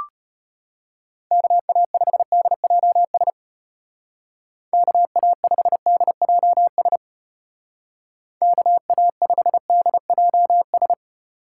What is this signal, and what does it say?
Morse code, heard as a keyed tone, about 35 wpm, 700 Hz. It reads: KA5DJS KA5DJS KA5DJS